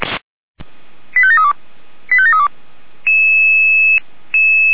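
Five-tone selective call (ZVEI), heard as quick runs of five tones, tones of about 70 ms. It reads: 85732 85732
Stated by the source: ZVEI five-tone selective call received on SDR.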